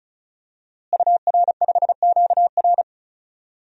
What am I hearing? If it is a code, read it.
Morse code, heard as a keyed tone, about 36 wpm, 700 Hz. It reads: UR5QR
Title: DXCC-100 Call Signs - 5 Characters - Repeated 3x - 35wpm